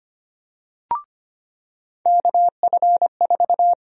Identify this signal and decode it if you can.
Morse code, heard as a keyed tone, about 25 wpm, 700 Hz. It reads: KF4
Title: USA Call Sign Prefixes - 25wpm